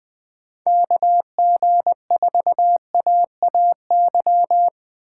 Morse code, heard as a keyed tone, about 24 wpm, 700 Hz. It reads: KG4AAY
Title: Speed Racing - USA 2x3 Call Signs 20wpm